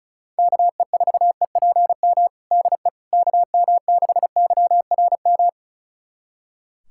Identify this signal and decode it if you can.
Morse code, heard as a keyed tone, about 34 wpm, 700 Hz. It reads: KE4EPM DE KM6YRM
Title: USA Callsign DE Callsign - 6 Characters - 35wpm